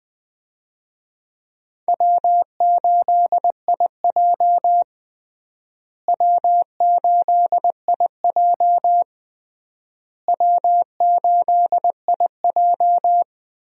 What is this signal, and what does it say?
Morse code, heard as a keyed tone, about 20 wpm, 700 Hz. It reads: W8IJ W8IJ W8IJ
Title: USA Call Signs - 4 Characters - Repeated 3x - 20wpm